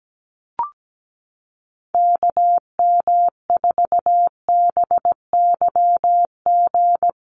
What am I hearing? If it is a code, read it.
Morse code, heard as a keyed tone, about 17 wpm, 700 Hz. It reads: KM4BYG
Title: USA Call Signs Repeated 3x 17wpm